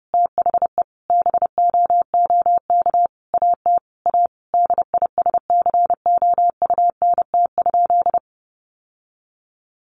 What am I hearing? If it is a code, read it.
Morse code, heard as a keyed tone, about 30 wpm, 700 Hz. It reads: THE BOOK AT A DISCOUNT?
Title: Sentences from Top 1200 Words 30wpm